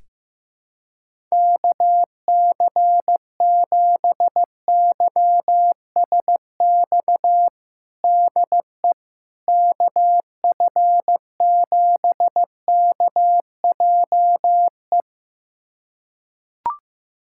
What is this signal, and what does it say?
Morse code, heard as a keyed tone, about 15 wpm, 700 Hz. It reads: KC7YSX DE KF7KJE
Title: USA Callsign DE Callsign - 6 Characters - 15wpm